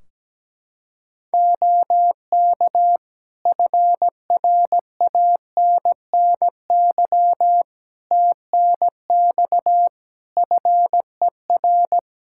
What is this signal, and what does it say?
Morse code, heard as a keyed tone, about 17 wpm, 700 Hz. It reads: OK FRANNY TNX FER
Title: Practice QSO - Exchange 4 - Ending - 17wpm